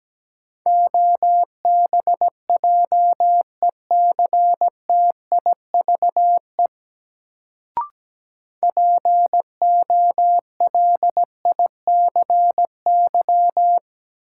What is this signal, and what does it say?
Morse code, heard as a keyed tone, about 17 wpm, 700 Hz. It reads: OBJECTIVE POLICY
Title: Sentences from Top 1200 Words - New Word Drill Alt - 17wpm